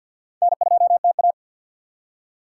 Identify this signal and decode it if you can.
Morse code, heard as a keyed tone, about 66 wpm, spaced as for 58 wpm, 700 Hz. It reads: N2TA
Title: Speed Racing - USA 1x2 Call Signs 50wpm